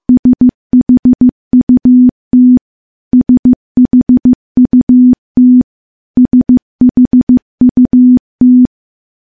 Morse code, heard as a keyed tone, about 15 wpm, 250 Hz. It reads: SHUT SHUT SHUT
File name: morse-training-15WPM